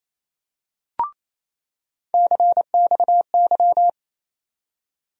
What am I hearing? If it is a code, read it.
Morse code, heard as a keyed tone, about 29 wpm, 700 Hz. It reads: CXY
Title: USA Call Sign Postfixes - 3 Letters 28wpm